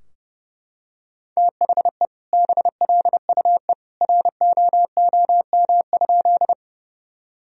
Morse code, heard as a keyed tone, about 30 wpm, 700 Hz. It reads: THE BLUE ROOM?
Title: Sentences from Top 1000 Words - Review Part 3 - 30wpm